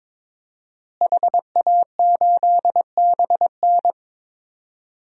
Morse code, heard as a keyed tone, about 22 wpm, 700 Hz. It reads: HA8BN
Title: DXCC-100 Call Signs - 5 Characters - Repeated 2x - 22wpm